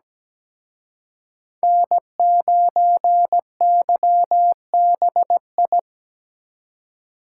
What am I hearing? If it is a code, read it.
Morse code, heard as a keyed tone, about 20 wpm, 700 Hz. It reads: N9YBI
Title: Speed Racing - USA 1x3 Call Signs 17wpm